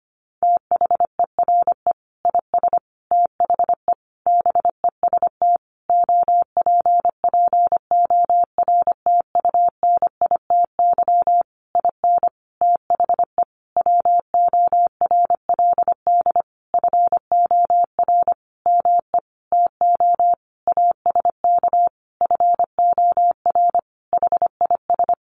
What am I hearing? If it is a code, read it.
Morse code, heard as a keyed tone, about 25 wpm, 700 Hz. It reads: THERE IS THE BEST OPPORTUNITY IN THE WORLD FOR ME TO ASK FOR HIS
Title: Sentences from Top 1500 Words 25wpm